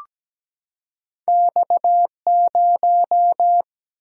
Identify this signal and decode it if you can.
Morse code, heard as a keyed tone, about 17 wpm, 700 Hz. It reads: X 0
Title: Single-Letter Single-Number 17wpm